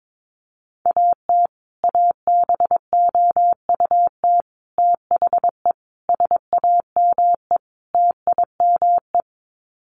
Morse code, heard as a keyed tone, about 22 wpm, 700 Hz. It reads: AT ABOUT THE SAME TIME
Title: Sets of 5 Words Encore 22wpm